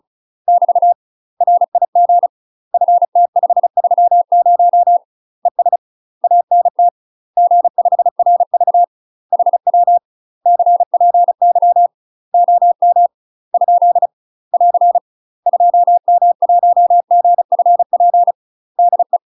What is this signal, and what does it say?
Morse code, heard as a keyed tone, about 36 wpm, 700 Hz. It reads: <BT> RIG FT530 ES ANT G5RV HW CPY OM ? <AR> 2M1GFP DE
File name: UK MASTER QSOs TRAINING / 35WPM / UK MASTER QSOs 0009